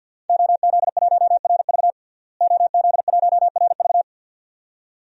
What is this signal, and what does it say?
Morse code, heard as a keyed tone, about 53 wpm, 700 Hz. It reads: OZ1RV OZ1RV
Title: DXCC-100 Call Signs - 5 Characters - Repeated 2x - 50wpm